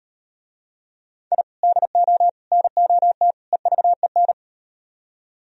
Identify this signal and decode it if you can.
Morse code, heard as a keyed tone, about 38 wpm, 700 Hz. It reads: I DO NOT EVEN